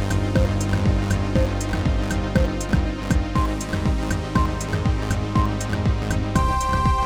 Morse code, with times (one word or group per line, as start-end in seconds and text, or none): none